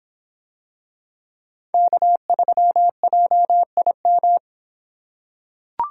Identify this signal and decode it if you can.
Morse code, heard as a keyed tone, about 24 wpm, 700 Hz. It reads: K3JIM